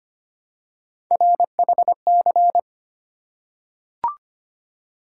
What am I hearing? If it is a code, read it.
Morse code, heard as a keyed tone, about 25 wpm, 700 Hz. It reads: RHC
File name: Three Letter-Number Combinations - 25wpm